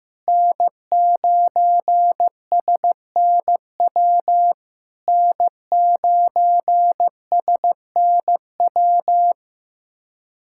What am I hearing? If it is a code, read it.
Morse code, heard as a keyed tone, about 15 wpm, 700 Hz. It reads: N9SNW N9SNW
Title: SOTA QSO Practice 15wpm